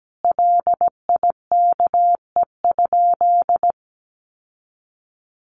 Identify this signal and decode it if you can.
Morse code, heard as a keyed tone, about 17 wpm, 700 Hz. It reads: LIKE?
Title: Sentences from Top 1500 Words 17wpm